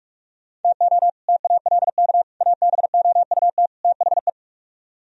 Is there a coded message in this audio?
Yes: Morse code from 0.6 to 4.3 s, heard as a keyed tone, about 47 wpm, 700 Hz. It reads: TO TALK ABOUT THE